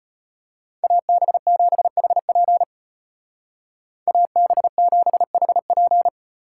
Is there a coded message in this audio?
Yes: Morse code from 0.8 to 6.1 s, heard as a keyed tone, about 33 wpm, 700 Hz. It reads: AB7HP AB7HP